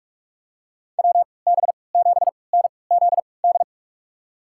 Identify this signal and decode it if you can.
Morse code, heard as a keyed tone, about 46 wpm, spaced as for 25 wpm, 700 Hz. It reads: WB7NZD
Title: USA Call Signs - 3x Character Spacing - 45wpm